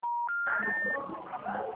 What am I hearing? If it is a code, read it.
Special information tone, with speech.